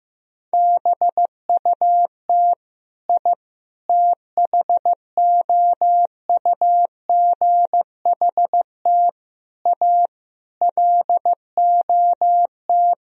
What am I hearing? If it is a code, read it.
Morse code, heard as a keyed tone, about 15 wpm, 700 Hz. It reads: BUT I THOUGHT A LOT
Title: Sentences from Top 1000 Words - Review Part 3 - 15wpm